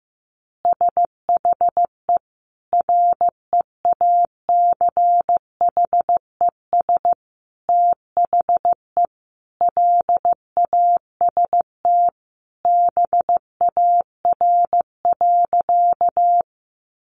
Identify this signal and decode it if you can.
Morse code, heard as a keyed tone, about 15 wpm, 700 Hz. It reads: SHE REACHES THE LAST BAR.
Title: Easy Phrases 15wpm